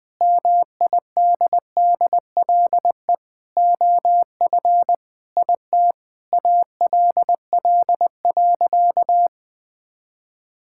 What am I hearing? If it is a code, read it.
Morse code, heard as a keyed tone, about 20 wpm, 700 Hz. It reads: MIDDLE OF IT ALL.
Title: Sentences from Top 1000 Words - Review Part 6 - 20wpm